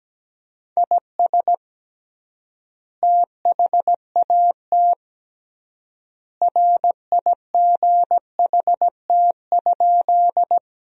Morse code, heard as a keyed tone, about 17 wpm, 700 Hz. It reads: IS THAT RIGHT?
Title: 3x Word-Spacing - Sentences from Top 200 Words - 17wpm